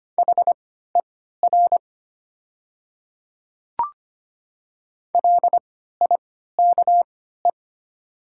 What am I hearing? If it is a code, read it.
Morse code, heard as a keyed tone, about 25 wpm, spaced as for 14 wpm, 700 Hz. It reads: HER LIKE
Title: 3x Character Spacing - Top 100 Words - 25wpm